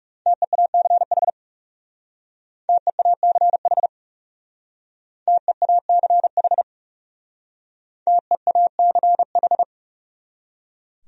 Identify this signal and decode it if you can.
Morse code, heard as a keyed tone, about 40 wpm, 700 Hz. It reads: TEACH TEACH TEACH TEACH